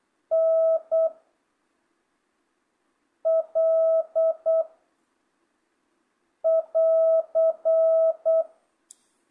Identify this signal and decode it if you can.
Morse code, heard as a keyed tone, about 8 wpm, 660 Hz. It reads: N L <AR>
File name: jh8cbh / 2025cw